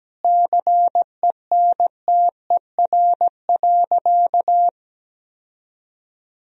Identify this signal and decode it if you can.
Morse code, heard as a keyed tone, about 17 wpm, 700 Hz. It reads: CENTER.
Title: Sentences from Top 1000 Words - Review Part 2 - 17wpm